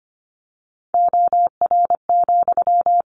Morse code, heard as a keyed tone, about 25 wpm, 700 Hz. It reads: OR,
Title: Sentences from Top 700 Words 25wpm